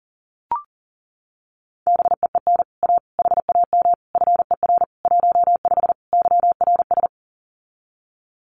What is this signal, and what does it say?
Morse code, heard as a keyed tone, about 37 wpm, 700 Hz. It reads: BEEN A HAM FER 15 YRS
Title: Common QSO Phrases Encore 40wpm